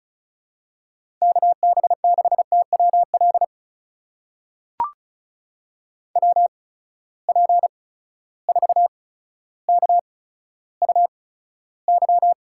Morse code, heard as a keyed tone, about 36 wpm, spaced as for 15 wpm, 700 Hz. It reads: KB6TWL WP4KUY
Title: USA Call Signs - 6 Characters - 8x Character Spacing - 35wpm